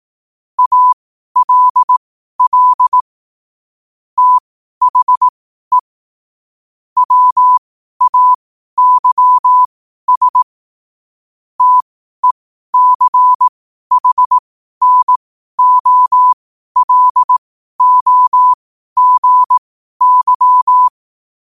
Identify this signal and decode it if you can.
Morse code, heard as a keyed tone, about 18 wpm, spaced as for 12 wpm, 1000 Hz. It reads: ALL THE WAYS TECHNOLOGY
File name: qotdfiles / 12wpm